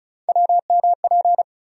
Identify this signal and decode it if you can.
Morse code, heard as a keyed tone, about 36 wpm, 700 Hz. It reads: WMP
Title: USA Call Sign Postfixes - 3 Letters 35wpm